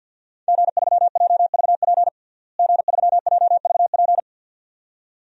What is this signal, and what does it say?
Morse code, heard as a keyed tone, about 53 wpm, 700 Hz. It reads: G3JVP G3JVP
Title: DXCC-100 Call Signs - 5 Characters - Repeated 2x - 50wpm